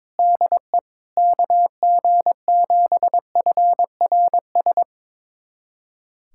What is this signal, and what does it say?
Morse code, heard as a keyed tone, about 21 wpm, 700 Hz. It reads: DE KG7FRS
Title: USA Callsign DE Callsign 22wpm